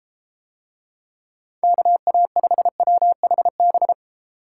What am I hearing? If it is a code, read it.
Morse code, heard as a keyed tone, about 32 wpm, 700 Hz. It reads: KA5WHB